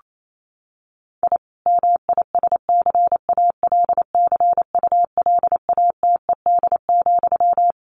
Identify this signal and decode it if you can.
Morse code, heard as a keyed tone, about 28 wpm, 700 Hz. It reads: I MISCALCULATED,
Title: A Sound Of Thunder 28wpm